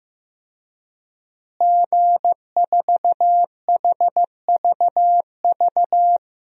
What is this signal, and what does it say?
Morse code, heard as a keyed tone, about 15 wpm, 700 Hz. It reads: G4HVV
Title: DXCC-100 Call Signs - 5 Characters - Repeated 3x - 15wpm